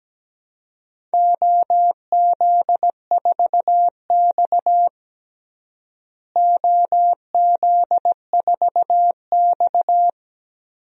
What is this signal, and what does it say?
Morse code, heard as a keyed tone, about 17 wpm, 700 Hz. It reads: OZ4X OZ4X
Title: DXCC-100 Call Signs - 4 Characters - Repeated 3x - 17wpm